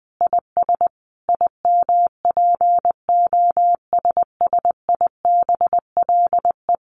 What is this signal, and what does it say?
Morse code, heard as a keyed tone, about 20 wpm, 700 Hz. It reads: IS IMPOSSIBLE
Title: Sentences from Top 1200 Words 20wpm